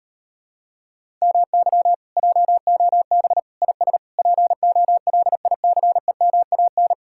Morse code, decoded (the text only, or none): MY JOB IS POLICEMAN